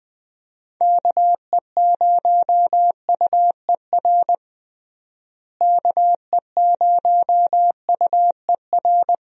Morse code, heard as a keyed tone, about 20 wpm, 700 Hz. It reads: KE0UER KE0UER
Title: USA Call Signs - 6 Characters - Repeated 3x - 20wpm